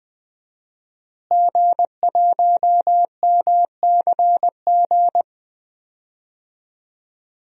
Morse code, heard as a keyed tone, about 21 wpm, 700 Hz. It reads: G1MCG
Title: DXCC-100 Call Signs - 5 Characters - Repeated 3x - 20wpm